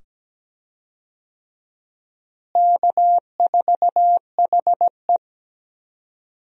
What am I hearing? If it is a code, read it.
Morse code, heard as a keyed tone, about 17 wpm, 700 Hz. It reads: K4HE
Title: USA Call Signs - 4 Characters - Repeated 2x - 17wpm